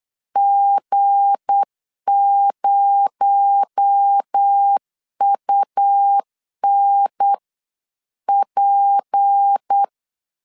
Morse code, heard as a keyed tone, about 8 wpm, 800 Hz. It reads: G0UNP
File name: LEZIONI / 7 GF9Z / 7 GF9 8wpm 3545